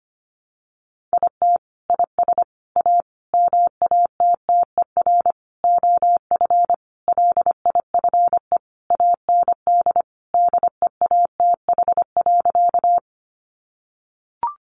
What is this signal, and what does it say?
Morse code, heard as a keyed tone, about 25 wpm, 700 Hz. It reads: IT IS A MATTER OF LIFE AND DEATH.
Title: Sentences from Top 1100 Words 25wpm